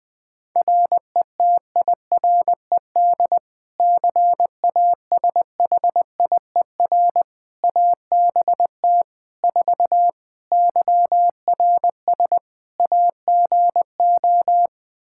Morse code, heard as a keyed tone, about 20 wpm, 700 Hz. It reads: RETIRED CASHIER ABT 4 YRS AGO
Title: Practice QSO - Exchange 3 - More Info - 20wpm